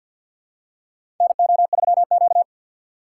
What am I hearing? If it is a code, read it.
Morse code, heard as a keyed tone, about 47 wpm, 700 Hz. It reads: NO3Q
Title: USA Call Signs - 4 Characters - 3x Character Spacing - 50wpm